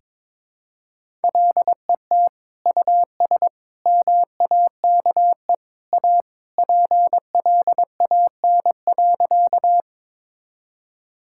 Morse code, heard as a keyed tone, about 21 wpm, 700 Hz. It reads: LET US MAKE A PLAN.
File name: practice / 22wpm / Sentences with 2-Word Phrases - 22wpm